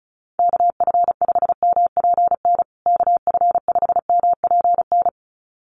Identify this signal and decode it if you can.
Morse code, heard as a keyed tone, about 34 wpm, 700 Hz. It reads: KF5MPN KF5MPN
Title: USA Call Signs Repeated 2x 35wpm